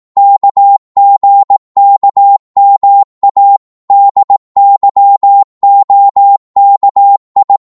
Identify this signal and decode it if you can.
Morse code, heard as a keyed tone, about 18 wpm, 800 Hz. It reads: KGKMA DYOKI